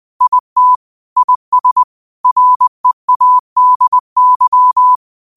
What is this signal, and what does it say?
Morse code, heard as a keyed tone, about 20 wpm, 1000 Hz. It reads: IT IS READY
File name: qotdfiles / 20wpm